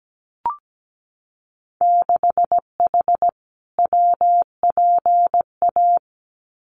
Morse code, heard as a keyed tone, about 17 wpm, 700 Hz. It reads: 6H WPA
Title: Winter Field Day Exchanges Encore 17wpm